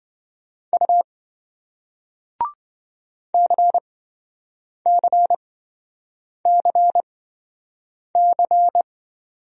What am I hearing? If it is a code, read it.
Morse code, heard as a keyed tone, about 27 wpm, spaced as for 7 wpm, 700 Hz. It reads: U CCCC